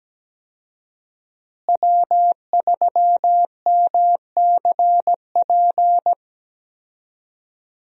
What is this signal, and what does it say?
Morse code, heard as a keyed tone, about 17 wpm, 700 Hz. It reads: W3MCP
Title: USA Call Signs - 5 Characters - Repeated 3x - 17wpm